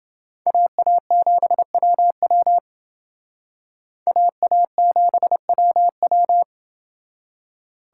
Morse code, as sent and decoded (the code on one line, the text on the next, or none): .- .- --... .-- .-- / .- .- --... .-- .--
AA7WW AA7WW